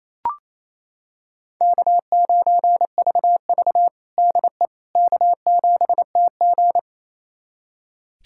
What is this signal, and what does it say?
Morse code, heard as a keyed tone, about 28 wpm, 700 Hz. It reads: K9VV DE K7TG
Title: USA Callsign DE Callsign - 4 Characters - 28wpm